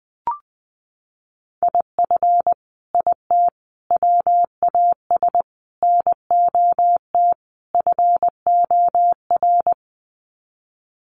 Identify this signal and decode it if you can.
Morse code, heard as a keyed tone, about 20 wpm, 700 Hz. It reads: IF IT WAS NOT FOR
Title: Sets of 5 Words Encore 20wpm